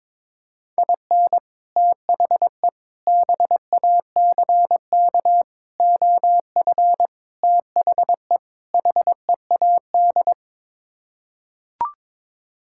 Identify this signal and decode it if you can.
Morse code, heard as a keyed tone, about 21 wpm, 700 Hz. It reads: IN THE BACK OF THE HEAD
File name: practice / 22wpm / Sets of 6 Words - 22wpm